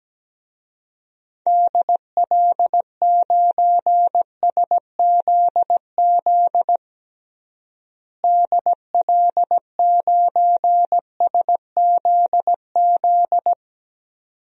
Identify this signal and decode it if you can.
Morse code, heard as a keyed tone, about 17 wpm, 700 Hz. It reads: DL9SZZ DL9SZZ
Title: DXCC-100 Call Signs - 6 Characters - Repeated 2x - 17wpm